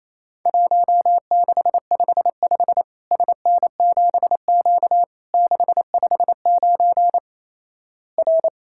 Morse code, heard as a keyed tone, about 28 wpm, 700 Hz. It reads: 1655 SN7Q 659 R
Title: Worked All Europe DX Contest - QTC Exchange - 28wpm